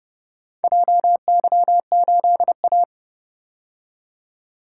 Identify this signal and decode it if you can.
Morse code, heard as a keyed tone, about 29 wpm, 700 Hz. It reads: JY8A